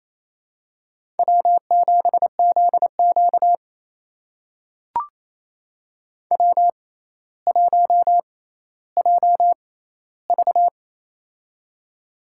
Morse code, heard as a keyed tone, about 28 wpm, spaced as for 14 wpm, 700 Hz. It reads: W7ZQ W1JV
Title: USA Call Signs - 4 Characters - 6x Character Spacing - 28wpm